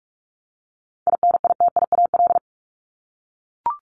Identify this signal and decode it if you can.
Morse code, heard as a keyed tone, about 44 wpm, 700 Hz. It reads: INITIAL
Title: Top 1501-1750 Words 45wpm